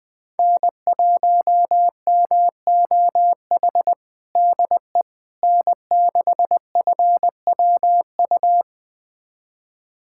Morse code, heard as a keyed tone, about 21 wpm, 700 Hz. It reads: N1MOH DE N6FWU
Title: USA Callsign DE Callsign - 5 Characters - 20wpm